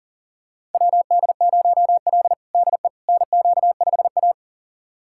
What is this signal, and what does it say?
Morse code, heard as a keyed tone, about 40 wpm, 700 Hz. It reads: WD0L DE NQ5A
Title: USA Callsign DE Callsign - 4 Characters - 40wpm